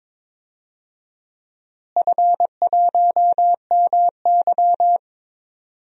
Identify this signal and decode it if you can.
Morse code, heard as a keyed tone, about 22 wpm, 700 Hz. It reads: F1MY
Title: DXCC-100 Call Signs - 4 Characters - Repeated 2x - 22wpm